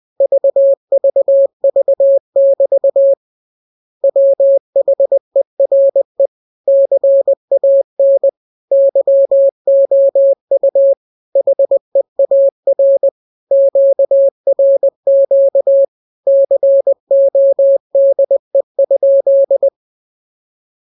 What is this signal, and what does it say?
Morse code, heard as a keyed tone, about 20 wpm, 550 Hz. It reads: VVV<BT> WHERE CAN YOU HEAR QRQ CODE?